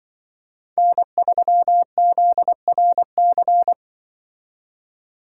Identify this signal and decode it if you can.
Morse code, heard as a keyed tone, about 23 wpm, 700 Hz. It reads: N3ZRC
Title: Speed Racing - USA 1x3 Call Signs 20wpm